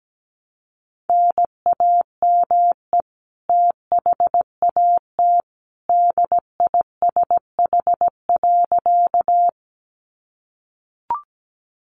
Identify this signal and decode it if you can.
Morse code, heard as a keyed tone, about 17 wpm, 700 Hz. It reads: NAME THAT DISH.
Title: Sentences from Top 1300 Words 17wpm